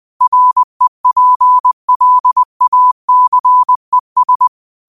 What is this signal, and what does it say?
Morse code, heard as a keyed tone, about 20 wpm, 1000 Hz. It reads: REPLACES
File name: qotdfiles / 20wpm